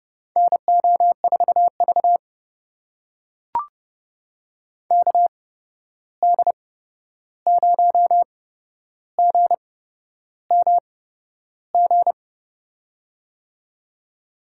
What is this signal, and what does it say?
Morse code, heard as a keyed tone, about 32 wpm, spaced as for 11 wpm, 700 Hz. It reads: NO4V KD0GMG
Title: USA Call Signs - 8x Character Spacing - 30wpm